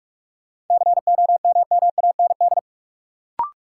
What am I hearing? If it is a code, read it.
Morse code, heard as a keyed tone, about 46 wpm, 700 Hz. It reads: COMMAND
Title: Sentences from Top 1600 Words - New Word Drill Alt - 45wpm